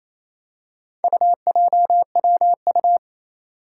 Morse code, heard as a keyed tone, about 28 wpm, 700 Hz. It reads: UJWU
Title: Four Challenging Letter-Number Combinations 28wpm